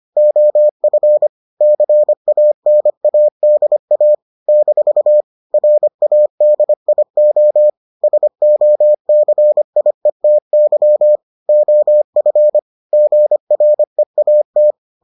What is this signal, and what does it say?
Morse code, heard as a keyed tone, about 25 wpm, 600 Hz. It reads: OF CANADA - RADIO SOCIETY OF GREAT